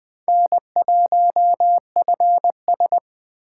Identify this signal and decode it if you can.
Morse code, heard as a keyed tone, about 21 wpm, 700 Hz. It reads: N1FS